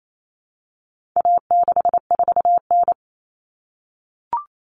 Morse code, heard as a keyed tone, about 28 wpm, 700 Hz. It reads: A64N
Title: Common Cut Numbers - 4-Digit 28wpm